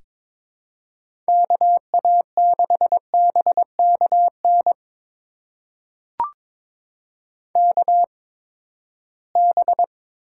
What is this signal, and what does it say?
Morse code, heard as a keyed tone, about 21 wpm, spaced as for 15 wpm, 700 Hz. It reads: KA6BKN KB
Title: USA Call Signs - 6 Characters - 8x Character Spacing - 22wpm